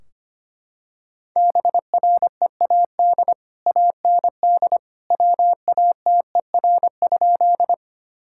Morse code, heard as a keyed tone, about 25 wpm, 700 Hz. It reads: BREAD AND WATER?
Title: Sentences from Top 1600 Words 25wpm